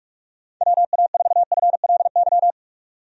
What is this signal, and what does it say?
Morse code, heard as a keyed tone, about 42 wpm, 700 Hz. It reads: WA4FLY
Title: USA Call Signs - 6 Characters - 45wpm